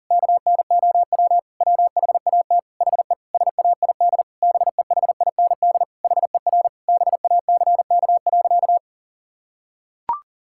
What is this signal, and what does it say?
Morse code, heard as a keyed tone, about 40 wpm, 700 Hz. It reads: KNOW WHAT HE SAID BEHIND HER BACK.
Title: Sentences from Top 1000 Words - Review Part 6 - 40wpm